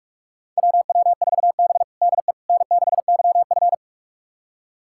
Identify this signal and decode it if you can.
Morse code, heard as a keyed tone, about 46 wpm, 700 Hz. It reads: WW4B DE N6YF
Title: USA Callsign DE Callsign - 4 Characters - 45wpm